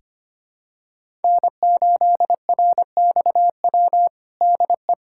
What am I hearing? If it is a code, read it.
Morse code, heard as a keyed tone, about 25 wpm, 700 Hz. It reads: N8RXW DE
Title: USA Callsign DE Callsign - 5 Characters - 25wpm